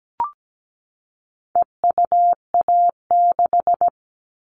Morse code, heard as a keyed tone, about 17 wpm, 700 Hz. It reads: EUA6
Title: All Cut Numbers - 4-Digit 17wpm